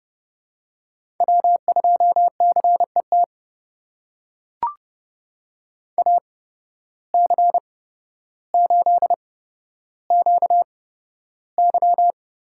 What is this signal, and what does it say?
Morse code, heard as a keyed tone, about 30 wpm, spaced as for 13 wpm, 700 Hz. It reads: W2CET AC8QY
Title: USA Call Signs - 8x Character Spacing - 30wpm